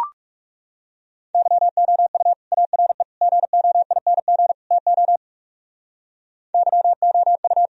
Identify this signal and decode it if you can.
Morse code, heard as a keyed tone, about 37 wpm, 700 Hz. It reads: YOU ARE GOING TO YOU
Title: Speed-Racing - Sets of 4 Words 30wpm